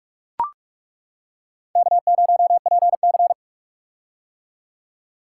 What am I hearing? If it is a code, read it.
Morse code, heard as a keyed tone, about 42 wpm, 700 Hz. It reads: K0PC
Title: K1USN SST QSO Practice - 45wpm